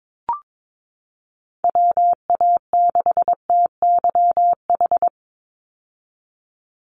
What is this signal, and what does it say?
Morse code, heard as a keyed tone, about 22 wpm, 700 Hz. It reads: WA6TYH
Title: USA Call Signs 22wpm